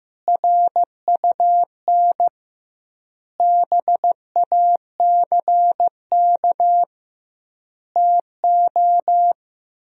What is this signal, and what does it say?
Morse code, heard as a keyed tone, about 15 wpm, 700 Hz. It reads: RUN BACK TO
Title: 2x Word-Spacing - Sentences from Top 300 Words - 15wpm